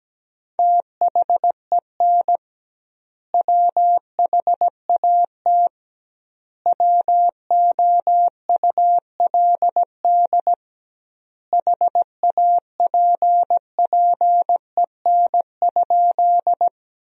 Morse code, heard as a keyed tone, about 17 wpm, 700 Hz. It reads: THEN WHAT WOULD HAPPEN?
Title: 2x Word-Spacing - Sentences from Top 400 Words - 17wpm